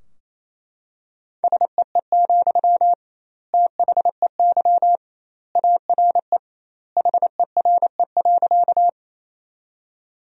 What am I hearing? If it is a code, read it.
Morse code, heard as a keyed tone, about 28 wpm, 700 Hz. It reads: SEE, THEY ARE HERE.
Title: 2x Word-Spacing - Sentences from Top 200 Words - 28wpm